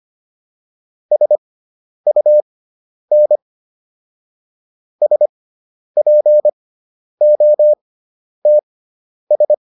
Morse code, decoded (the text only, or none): SUN SPOTS